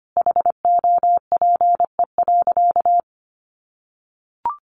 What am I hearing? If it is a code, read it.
Morse code, heard as a keyed tone, about 25 wpm, 700 Hz. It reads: HOPE.
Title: Sentences from Top 1000 Words 25wpm